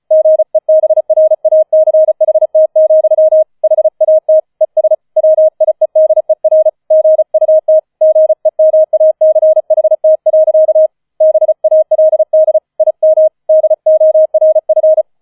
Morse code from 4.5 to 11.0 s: ES WIEDER GUT GEMACHT.